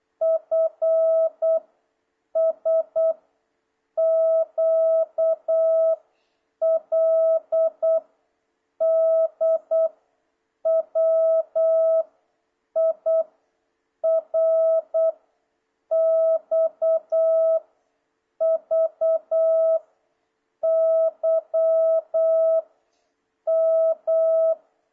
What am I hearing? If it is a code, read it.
Morse code, heard as a keyed tone, about 8 wpm, 650 Hz. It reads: F S Q L D W I R X V Y M
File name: jh8cbh / 2026cw